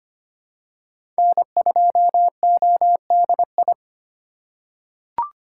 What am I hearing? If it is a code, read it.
Morse code, heard as a keyed tone, about 25 wpm, 700 Hz. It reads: N2ODI
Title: USA Call Signs - 5 Characters - 4x Character Spacing - 25wpm